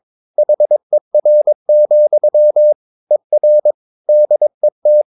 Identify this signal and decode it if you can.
Morse code, heard as a keyed tone, about 22 wpm, 600 Hz. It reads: HER, ER DET
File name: Ferie QRS 22wpm